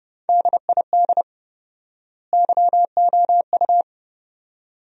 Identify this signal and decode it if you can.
Morse code, heard as a keyed tone, about 30 wpm, 700 Hz. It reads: DID YOU
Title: 4x Word-Spacing - Sentences from Top 200 Words - 30wpm